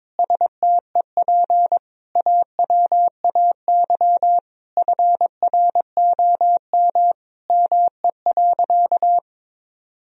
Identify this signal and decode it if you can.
Morse code, heard as a keyed tone, about 22 wpm, 700 Hz. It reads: STEP AWAY FROM ME.
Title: Sentences from Top 1000 Words - Review Part 2 - 22wpm